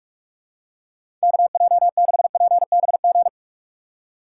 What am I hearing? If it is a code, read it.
Morse code, heard as a keyed tone, about 46 wpm, 700 Hz. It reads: KJ6PBG
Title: USA Call Signs - 6 Characters - 4x Character Spacing - 45wpm